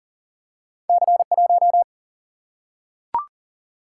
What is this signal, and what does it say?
Morse code, heard as a keyed tone, about 39 wpm, 700 Hz. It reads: C1
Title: Single-Letter Single-Number 40wpm